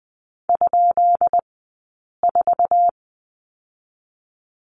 Morse code, heard as a keyed tone, about 20 wpm, 700 Hz. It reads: ? 4
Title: 204 Lesson 33d - Letter j - Two Characters 20wpm